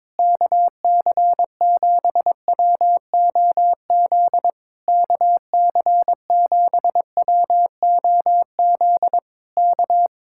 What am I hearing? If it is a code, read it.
Morse code, heard as a keyed tone, about 22 wpm, 700 Hz. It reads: KC7WOZ KC7WOZ K
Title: Speed-Racing Common QSO Phrases - 22wpm